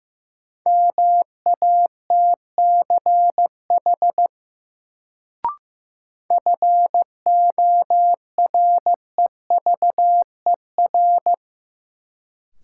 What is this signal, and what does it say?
Morse code, heard as a keyed tone, about 15 wpm, 700 Hz. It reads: MATCH FOREVER
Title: Sentences from Top 1600 Words - New Word Drill Alt - 15wpm